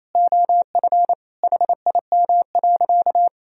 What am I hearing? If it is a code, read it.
Morse code, heard as a keyed tone, about 28 wpm, 700 Hz. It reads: OF HIM.
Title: Sentences from Top 1000 Words - Review Part 3 - 28wpm